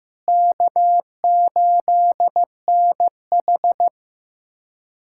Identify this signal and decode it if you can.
Morse code, heard as a keyed tone, about 15 wpm, 700 Hz. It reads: K8NH
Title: USA Callsign DE Callsign - 4 Characters - 15wpm